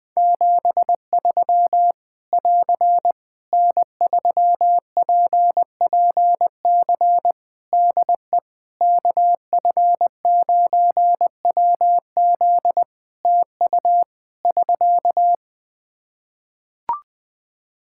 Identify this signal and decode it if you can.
Morse code, heard as a keyed tone, about 20 wpm, 700 Hz. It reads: 73 <AR> N3PPC DE KF9WZ TU <SK>
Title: K1USN SST QSO Practice 20wpm